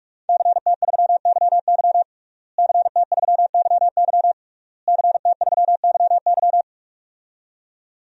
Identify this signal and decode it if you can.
Morse code, heard as a keyed tone, about 46 wpm, 700 Hz. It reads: CT3YY CT3YY CT3YY
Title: DXCC-100 Call Signs - 5 Characters - Repeated 3x - 45wpm